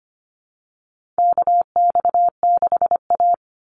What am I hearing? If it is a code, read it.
Morse code, heard as a keyed tone, about 25 wpm, 700 Hz. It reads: KX6A
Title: NA SOTA Chaser Callsigns 25wpm